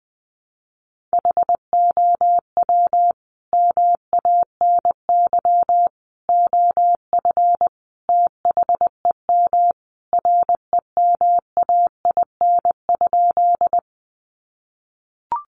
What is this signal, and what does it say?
Morse code, heard as a keyed tone, about 20 wpm, 700 Hz. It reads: HOW MANY OF THEM REMAIN?
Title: Sentences from Top 1200 Words 20wpm